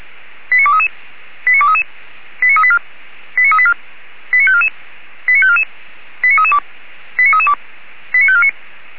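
Five-tone selective call (ZVEI), heard as quick runs of five tones, tones of about 70 ms. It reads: 89233 89233 89384 89384 89544 89544 89332 89332 89549
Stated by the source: ZVEI five-tone call received by SDR, from the Feuerwehr Orlamünde fire service.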